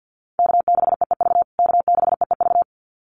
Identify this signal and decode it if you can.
Morse code, heard as a keyed tone, about 49 wpm, 700 Hz. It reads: K6EEV K6EEV
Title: USA Call Signs Repeated 2x 50wpm